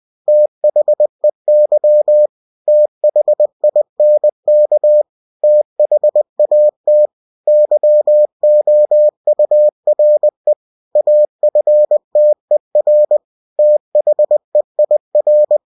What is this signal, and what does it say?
Morse code, heard as a keyed tone, about 20 wpm, 600 Hz. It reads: THEY THINK THAT YOURE AFTER THEIR
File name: cw-books / winnie-the-pooh / winnie-20wpm-page-0007